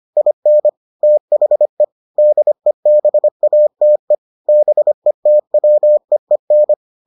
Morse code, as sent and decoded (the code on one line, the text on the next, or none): .. -. / - .... / -.. -... .- - . / -... - .-- . -.
IN THE DEBATE BETWEEN